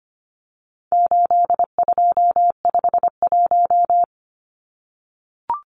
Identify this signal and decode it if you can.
Morse code, heard as a keyed tone, about 25 wpm, 700 Hz. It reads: 8251
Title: Numbers - 4-Digit 25wpm